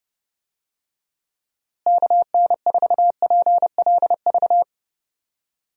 Morse code, heard as a keyed tone, about 31 wpm, 700 Hz. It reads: KN4PLV
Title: USA Call Signs - 6 Characters - Repeated 3x - 30wpm